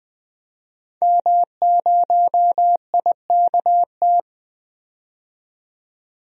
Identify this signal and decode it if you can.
Morse code, heard as a keyed tone, about 21 wpm, 700 Hz. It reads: M0IKT
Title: DXCC-100 Call Signs - 5 Characters - 20wpm